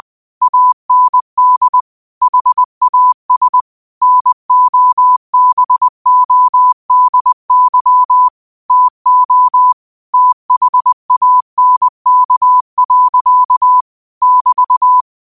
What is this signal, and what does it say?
Morse code, heard as a keyed tone, about 20 wpm, 1000 Hz. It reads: AND HAS NOBODY TO THANK. <BT>
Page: Quotes for Fri, 15 Aug 2025 in Morse Code at 20 words per minute.